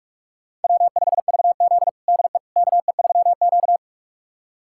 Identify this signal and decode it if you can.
Morse code, heard as a keyed tone, about 41 wpm, 700 Hz. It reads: W5VZ DE KE3Q
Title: USA Callsign DE Callsign - 4 Characters - 45wpm